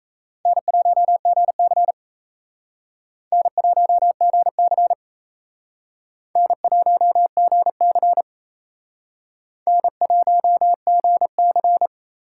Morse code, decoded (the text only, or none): N1GC N1GC N1GC N1GC